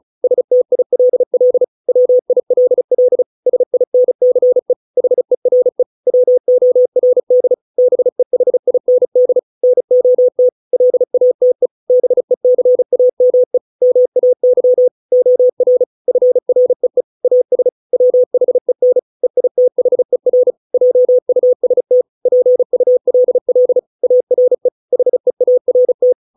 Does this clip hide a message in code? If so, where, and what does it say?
Yes: Morse code from 0.2 to 26.1 s, heard as a keyed tone, about 34 wpm, 500 Hz. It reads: STILL WILL SINCE HERE WORD BEHIND NOT LATE BECAME MAY OR FREE AS WHEN EITHER JUST PULL ARE HEART